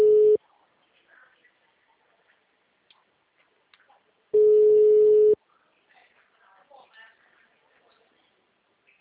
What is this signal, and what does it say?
Ringback tone (Europe, 425 Hz), with speech.
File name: out- -2019- - - .47621